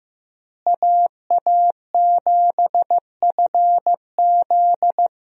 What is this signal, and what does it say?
Morse code, heard as a keyed tone, about 15 wpm, 700 Hz. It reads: AA7FZ
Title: USA Call Signs - 5 Characters - Repeated 2x - 15wpm